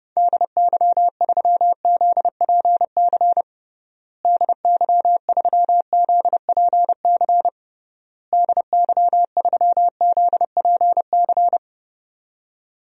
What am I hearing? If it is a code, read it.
Morse code, heard as a keyed tone, about 30 wpm, 700 Hz. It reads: DY3ZPC DY3ZPC DY3ZPC
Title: DXCC-100 Call Signs - 6 Characters - Repeated 3x - 30wpm